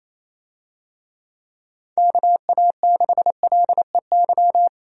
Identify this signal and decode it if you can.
Morse code, heard as a keyed tone, about 27 wpm, 700 Hz. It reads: KA6LEY
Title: USA Call Signs - 6 Characters - Repeated 2x - 28wpm